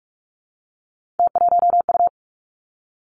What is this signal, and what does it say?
Morse code, heard as a keyed tone, about 46 wpm, 700 Hz. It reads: T1U
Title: Three Challenging Letter-Number Combinations 45wpm